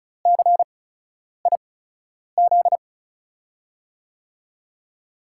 Morse code, heard as a keyed tone, about 36 wpm, spaced as for 10 wpm, 700 Hz. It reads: CIZ
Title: USA Call Signs - 6 Characters - 8x Character Spacing - 35wpm